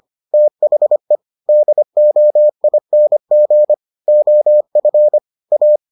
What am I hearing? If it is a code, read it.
Morse code, heard as a keyed tone, about 25 wpm, 600 Hz. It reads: THE DOING OF A